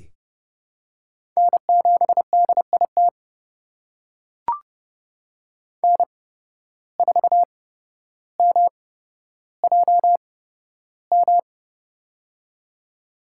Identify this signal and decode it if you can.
Morse code, heard as a keyed tone, about 30 wpm, spaced as for 13 wpm, 700 Hz. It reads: N7DIT N4MJM
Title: USA Call Signs - 8x Character Spacing - 30wpm